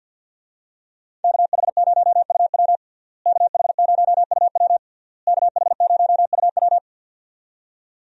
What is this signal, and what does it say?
Morse code, heard as a keyed tone, about 50 wpm, 700 Hz. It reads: KH0UW KH0UW KH0UW
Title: USA Call Signs - 5 Characters - Repeated 3x - 50wpm